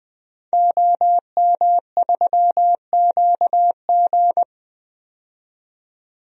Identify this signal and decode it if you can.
Morse code, heard as a keyed tone, about 21 wpm, 700 Hz. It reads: OM3QG
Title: DXCC-100 Call Signs - 5 Characters - 20wpm